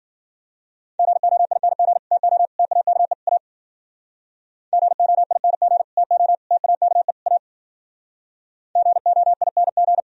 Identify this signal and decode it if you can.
Morse code, heard as a keyed tone, about 52 wpm, 700 Hz. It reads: GOING TO TAKE A GOING TO TAKE A GOING